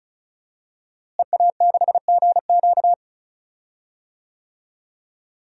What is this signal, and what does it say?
Morse code, heard as a keyed tone, about 34 wpm, 700 Hz. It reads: EA6GQ
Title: DXCC-100 Call Signs - 5 Characters - Repeated 2x - 35wpm